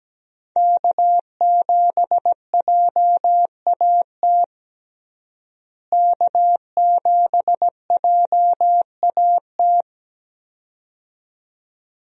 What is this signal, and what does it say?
Morse code, heard as a keyed tone, about 17 wpm, 700 Hz. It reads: K7JAT K7JAT
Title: USA Call Signs - 5 Characters - Repeated 2x - 17wpm